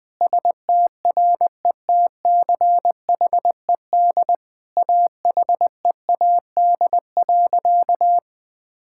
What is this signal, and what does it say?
Morse code, heard as a keyed tone, about 20 wpm, 700 Hz. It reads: STRETCHED AHEAD.
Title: Sentences with 2-Word Phrases - Spoken Beginning - 20wpm